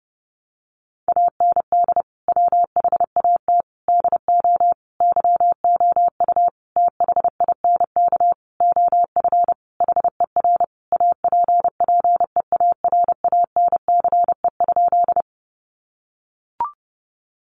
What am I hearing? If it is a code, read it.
Morse code, heard as a keyed tone, about 30 wpm, 700 Hz. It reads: AND WHAT DO YOU THINK OF HER APPEARANCE?
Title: Sentences from Top 1100 Words 30wpm